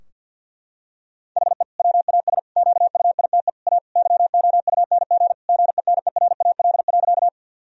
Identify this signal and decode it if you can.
Morse code, heard as a keyed tone, about 52 wpm, 700 Hz. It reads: HE WAS QUITE A YOUNG GENERAL.
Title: Sentences from Top 1000 Words - Review Part 3 - 50wpm